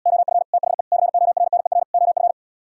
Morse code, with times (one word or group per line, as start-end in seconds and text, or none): 0.1-0.4 OF
0.5-0.8 THE
0.9-1.8 CORNER
1.9-2.3 OF